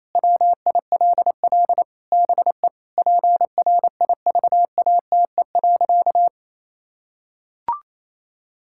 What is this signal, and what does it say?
Morse code, heard as a keyed tone, about 28 wpm, 700 Hz. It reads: WILL BE PRIVATE.
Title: Sentences from Top 1600 Words 28wpm